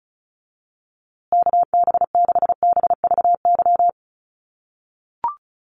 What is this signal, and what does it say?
Morse code, heard as a keyed tone, about 34 wpm, 700 Hz. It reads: KB6BVY
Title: USA Call Signs Repeated 2x 35wpm